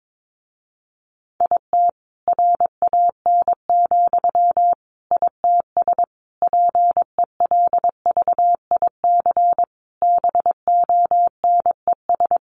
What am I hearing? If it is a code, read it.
Morse code, heard as a keyed tone, about 22 wpm, 700 Hz. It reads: IT RAN, ITS PELVIC BONES
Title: A Sound Of Thunder 22wpm